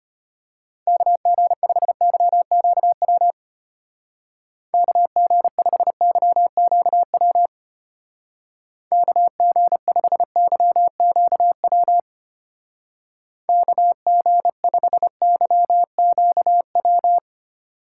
Morse code, heard as a keyed tone, about 31 wpm, 700 Hz. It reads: KG5YQW KG5YQW KG5YQW KG5YQW
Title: Speed Racing - USA 2x3 Call Signs 25wpm